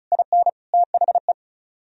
Morse code, heard as a keyed tone, about 46 wpm, 700 Hz. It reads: IN THE